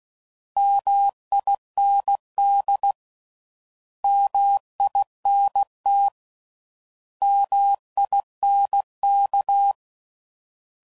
Morse code, heard as a keyed tone, about 16 wpm, 800 Hz. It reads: MIND MINT MINK